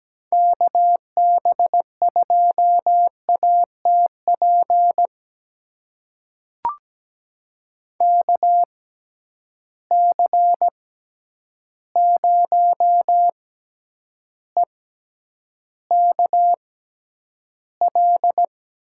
Morse code, heard as a keyed tone, about 17 wpm, spaced as for 9 wpm, 700 Hz. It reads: KB2ATP KC0EKL
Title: USA Call Signs - 6 Characters - 6x Character Spacing - 17wpm